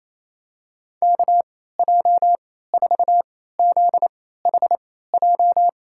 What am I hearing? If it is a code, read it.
Morse code, heard as a keyed tone, about 28 wpm, spaced as for 16 wpm, 700 Hz. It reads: KJ4ZHJ
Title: USA Call Signs - 6 Characters - 3x Character Spacing - 28wpm